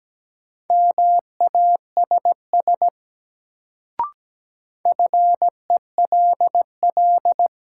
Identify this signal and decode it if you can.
Morse code, heard as a keyed tone, about 17 wpm, 700 Hz. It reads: MASS FELL
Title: Sets of 5 Words Encore - New Word Drill - 17wpm